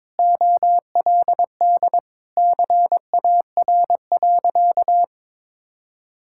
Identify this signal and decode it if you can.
Morse code, heard as a keyed tone, about 22 wpm, 700 Hz. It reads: OLD CAR.
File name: practice / 22wpm / Sentences with 8-Word Phrases - 22wpm